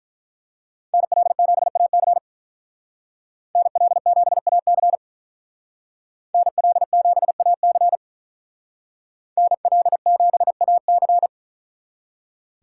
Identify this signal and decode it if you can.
Morse code, heard as a keyed tone, about 44 wpm, 700 Hz. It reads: NL7AC NL7AC NL7AC NL7AC